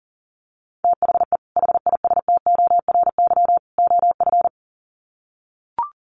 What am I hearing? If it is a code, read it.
Morse code, heard as a keyed tone, about 40 wpm, 700 Hz. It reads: THE HISTORY OF
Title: Sets of 3 Words 40wpm